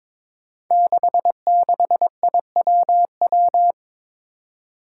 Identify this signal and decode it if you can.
Morse code, heard as a keyed tone, about 21 wpm, 700 Hz. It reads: 66IWW
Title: Five Challenging Letter-Number Combinations 22wpm